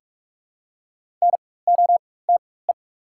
Morse code, heard as a keyed tone, about 42 wpm, spaced as for 21 wpm, 690 Hz. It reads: NOTE
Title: 4x Character Spacing - Top 500 Words - 45wpm